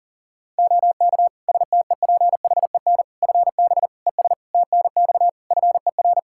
Morse code, heard as a keyed tone, about 40 wpm, 700 Hz. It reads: OK STEPHEN FB ES TNX FER
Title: Practice QSO - Exchange 2 - Info - 40wpm